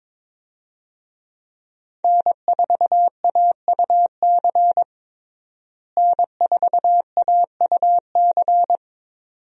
Morse code, heard as a keyed tone, about 22 wpm, 700 Hz. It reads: N4AUC N4AUC
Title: USA Call Signs - 5 Characters - Repeated 2x - 22wpm